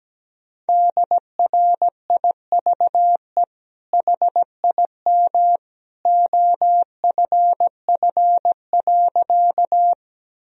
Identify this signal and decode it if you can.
Morse code, heard as a keyed tone, about 17 wpm, 700 Hz. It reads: DRIVE HIM OFF.
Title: Sentences from Top 1000 Words - Review Part 2 - 17wpm